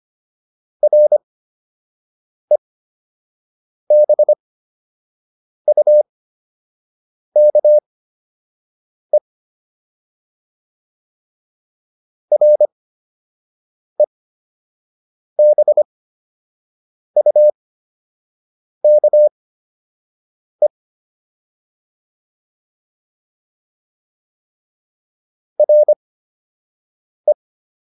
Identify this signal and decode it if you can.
Morse code, heard as a keyed tone, about 25 wpm, spaced as for 6 wpm, 600 Hz. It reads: REBUKE REBUKE RE